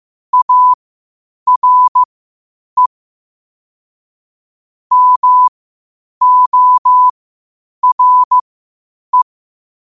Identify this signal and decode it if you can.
Morse code, heard as a keyed tone, about 15 wpm, spaced as for 8 wpm, 1000 Hz. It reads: ARE MORE
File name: qotdfiles / 8wpm